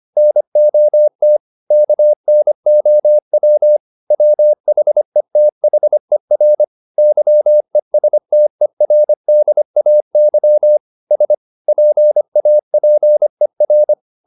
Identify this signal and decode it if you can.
Morse code, heard as a keyed tone, about 25 wpm, 600 Hz. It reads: NOT KNOW WHETHER YESTERDAY S PAPER
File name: the old man and the sea part 1 25wpm